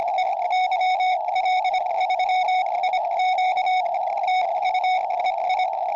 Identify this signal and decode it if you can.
Morse code, heard as a keyed tone, about 25 wpm, spaced as for 21 wpm, 2200 Hz, with radio noise. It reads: E YL3IQ TU EI
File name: tropo / yl3iq